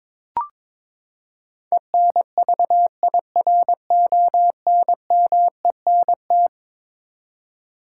Morse code, heard as a keyed tone, about 22 wpm, 700 Hz. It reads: ENVIRONMENT
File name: practice / 22wpm / Eleven Letter Words - 22wpm